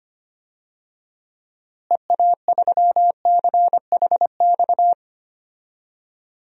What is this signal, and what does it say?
Morse code, heard as a keyed tone, about 25 wpm, 700 Hz. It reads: EA3CHX
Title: DXCC-100 Call Signs - 6 Characters - Repeated 2x - 25wpm